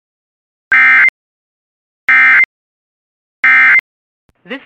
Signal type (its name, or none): Emergency Alert System (SAME) tones